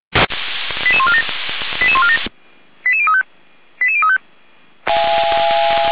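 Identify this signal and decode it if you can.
Five-tone selective call (ZVEI), heard as quick runs of five tones, tones of about 70 ms. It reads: 88147 88147 88035 88035
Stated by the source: Five-tone ZVEI selective call received by SDR, archived by the fire brigade Feuerwehr Orlamünde.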